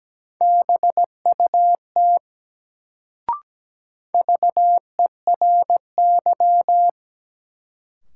Morse code, heard as a keyed tone, about 17 wpm, 700 Hz. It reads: BUT VERY
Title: Sets of 2 Words - New Word Drill - 17wpm